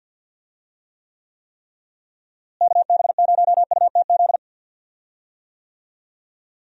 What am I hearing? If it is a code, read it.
Morse code, heard as a keyed tone, about 52 wpm, 700 Hz. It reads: KB0UTZ
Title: USA Call Signs - 6 Characters - Repeated 3x - 50wpm